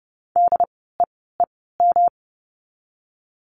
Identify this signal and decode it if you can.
Morse code, heard as a keyed tone, about 30 wpm, spaced as for 17 wpm, 700 Hz. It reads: DEEM
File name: course / 30:10wpm / 108 Lesson 18f - Letter m - Words 30wpm